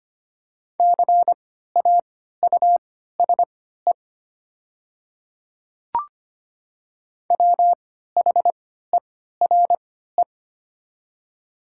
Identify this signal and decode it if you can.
Morse code, heard as a keyed tone, about 25 wpm, spaced as for 14 wpm, 700 Hz. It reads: CAUSE WHERE
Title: 3x Character Spacing - Top 300 Words - 25wpm